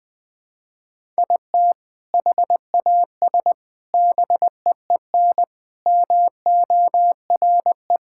Morse code, heard as a keyed tone, about 21 wpm, 700 Hz. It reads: IT HAS BEEN MORE
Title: Sentences with 4-Word Phrases - 20wpm